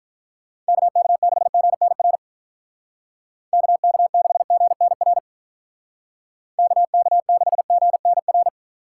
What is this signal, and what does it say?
Morse code, heard as a keyed tone, about 43 wpm, 700 Hz. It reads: KK6GNR KK6GNR KK6GNR
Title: Speed Racing - USA 2x3 Call Signs 35wpm